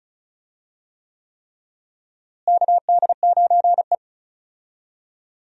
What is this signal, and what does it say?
Morse code, heard as a keyed tone, about 36 wpm, 700 Hz. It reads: KD9E
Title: USA Call Signs - 4 Characters - Repeated 3x - 35wpm